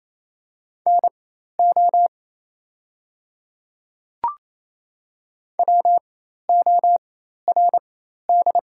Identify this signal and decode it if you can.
Morse code, heard as a keyed tone, about 27 wpm, spaced as for 13 wpm, 690 Hz. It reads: NO WORD
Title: 4x Character Spacing - Top 100 Words - 28wpm